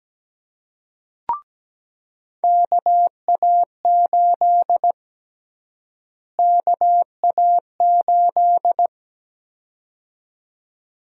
Morse code, heard as a keyed tone, about 17 wpm, 700 Hz. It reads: KA8 KA8
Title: USA Call Sign Prefixes - Repeated 2x - 17wpm